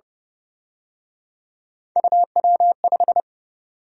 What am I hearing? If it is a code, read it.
Morse code, heard as a keyed tone, about 29 wpm, 700 Hz. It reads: UW5
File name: Speed-Racing - Three Letter-Number Combinations - 25wpm